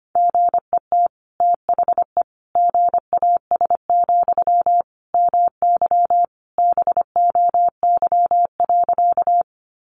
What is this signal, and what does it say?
Morse code, heard as a keyed tone, about 25 wpm, 700 Hz. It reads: GET THE GAS, MY BOY.
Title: Sentences from Top 1100 Words 25wpm